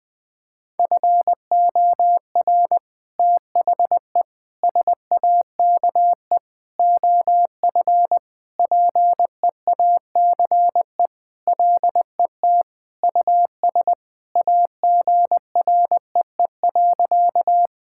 Morse code, heard as a keyed tone, about 20 wpm, 700 Hz. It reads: FOR THE SAKE OF PEACE LET US AGREE.
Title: Sentences with 4-Word Phrases - Spoken Beginning - 20wpm